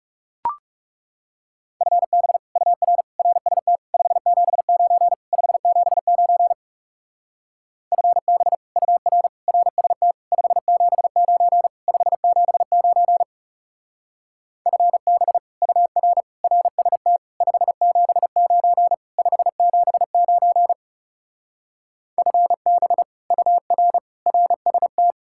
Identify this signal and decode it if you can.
Morse code, heard as a keyed tone, about 38 wpm, 700 Hz. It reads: FB UR RST 579 579 FB UR RST 579 579 FB UR RST 579 579 FB UR RST